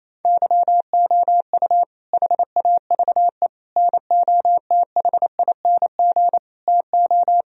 Morse code, heard as a keyed tone, about 28 wpm, 700 Hz. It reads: YOU HAVE NOTHING TO
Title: Sentences from Top 1000 Words - Review Part 1 - 28wpm